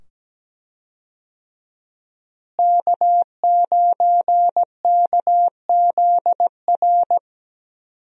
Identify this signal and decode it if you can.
Morse code, heard as a keyed tone, about 17 wpm, 700 Hz. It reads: K9KZR
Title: USA Call Signs - 5 Characters - Repeated 2x - 17wpm